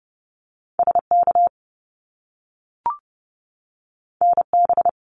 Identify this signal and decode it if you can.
Morse code, heard as a keyed tone, about 30 wpm, 700 Hz. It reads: SK NB
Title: Canadian Provinces and Territories 30wpm